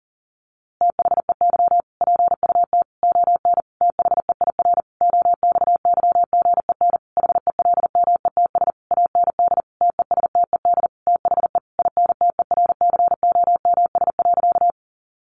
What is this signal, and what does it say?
Morse code, heard as a keyed tone, about 40 wpm, 700 Hz. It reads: THEY PUT ON THEIR OXYGEN HELMETS AND TESTED THE INTERCOMS.
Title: A Sound Of Thunder 40wpm